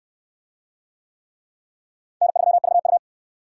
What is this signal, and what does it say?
Morse code, heard as a keyed tone, about 62 wpm, 700 Hz. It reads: N3VU